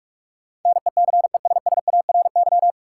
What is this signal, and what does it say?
Morse code, heard as a keyed tone, about 42 wpm, 700 Hz. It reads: NECESSARY
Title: Sentences from Top 800 Words - New Word Drill - 45wpm